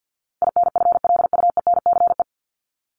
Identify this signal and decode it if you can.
Morse code, heard as a keyed tone, about 50 wpm, 700 Hz. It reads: INFLUENCE
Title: Top 1001-1250 Words 50wpm